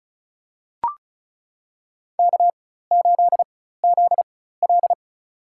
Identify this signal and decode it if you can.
Morse code, heard as a keyed tone, about 34 wpm, spaced as for 16 wpm, 700 Hz. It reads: K8ZL
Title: USA Call Signs - 4 Characters - 4x Character Spacing - 35wpm